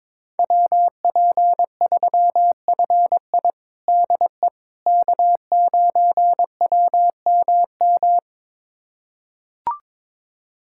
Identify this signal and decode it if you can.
Morse code, heard as a keyed tone, about 22 wpm, 700 Hz. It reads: WP3FI DE K9WMM
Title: USA Callsign DE Callsign - 5 Characters - 22wpm